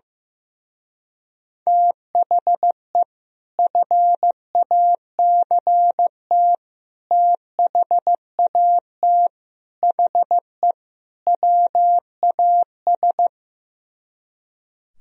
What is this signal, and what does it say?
Morse code, heard as a keyed tone, about 15 wpm, 700 Hz. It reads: THE FACT THAT HE WAS
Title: Speed-Racing - Sets of 5 Words 15wpm